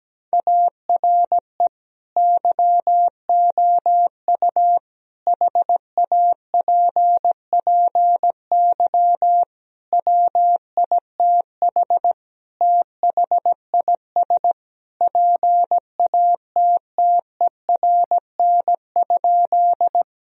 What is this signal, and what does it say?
Morse code, heard as a keyed tone, about 17 wpm, 700 Hz. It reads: ARE YOU HAPPY WITH THIS PATTERN?
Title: Sentences from Top 1000 Words - Review Part 2 - 17wpm